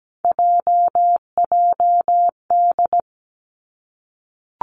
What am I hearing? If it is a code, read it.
Morse code, heard as a keyed tone, about 17 wpm, 700 Hz. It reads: JJD
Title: Three Challenging Letter-Number Combinations 17wpm